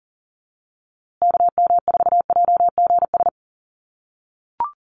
Morse code, heard as a keyed tone, about 40 wpm, 700 Hz. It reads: KM4JGS
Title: USA Call Signs Repeated 3x 40wpm